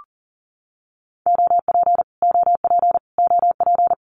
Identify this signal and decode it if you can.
Morse code, heard as a keyed tone, about 40 wpm, 700 Hz. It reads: OP OP OP
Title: Abbreviations Repeated 3x - 40wpm